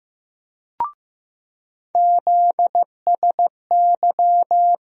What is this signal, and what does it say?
Morse code, heard as a keyed tone, about 15 wpm, 700 Hz. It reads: ZSY
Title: USA Call Sign Postfixes - 3 Letters 15wpm